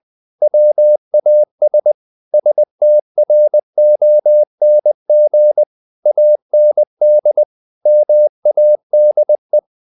Morse code, heard as a keyed tone, about 20 wpm, 600 Hz. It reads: WAS STRONG AND MADE
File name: the old man and the sea / the old man and the sea part 1 20wpm